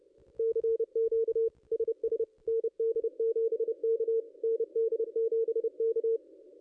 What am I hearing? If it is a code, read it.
Morse code, heard as a keyed tone, about 30 wpm, 440 Hz. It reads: CQ SS ND7K ND7K
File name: contests / ss-cw-k5tr-op-n5rz -2200